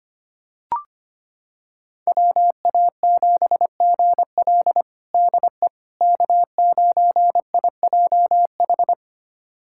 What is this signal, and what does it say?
Morse code, heard as a keyed tone, about 25 wpm, 700 Hz. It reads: WA7GL DE K9IJH
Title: USA Callsign DE Callsign - 5 Characters - 25wpm